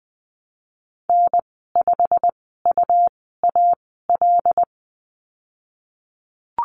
Morse code, heard as a keyed tone, about 20 wpm, spaced as for 15 wpm, 700 Hz. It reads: N5UAL
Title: 122 Lesson 20g - Number 3 - Call Signs 20wpm